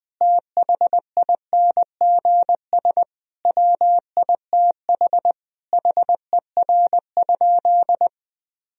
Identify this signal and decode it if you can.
Morse code, heard as a keyed tone, about 20 wpm, 700 Hz. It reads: THINGS WITH HER?
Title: Sentences from Top 1000 Words - Review Part 4 - 20wpm